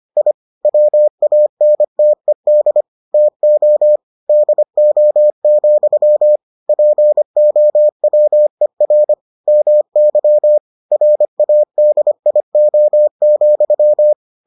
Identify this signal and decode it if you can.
Morse code, heard as a keyed tone, about 25 wpm, 600 Hz. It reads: I WANTED TO DO, POWER MY RADIO,